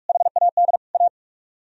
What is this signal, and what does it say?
Morse code, heard as a keyed tone, about 59 wpm, spaced as for 52 wpm, 700 Hz. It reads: HAD A
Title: Speed-Racing - Sets of 3 Words 45wpm